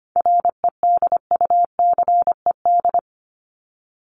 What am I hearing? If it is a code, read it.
Morse code, heard as a keyed tone, about 25 wpm, 700 Hz. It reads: REDUCED
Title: ED Suffix - Top 100 Words with Base Words 25wpm